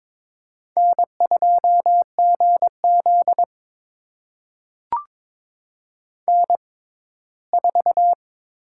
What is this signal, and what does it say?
Morse code, heard as a keyed tone, about 23 wpm, spaced as for 15 wpm, 700 Hz. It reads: N2GZ N4
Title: USA Call Signs - 4 Characters - 6x Character Spacing - 22wpm